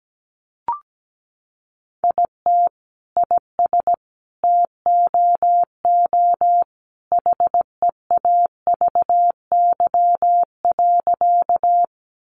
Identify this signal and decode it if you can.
Morse code, heard as a keyed tone, about 17 wpm, 700 Hz. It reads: IT IS TOO HEAVY.
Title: Sentences from Top 1100 Words 17wpm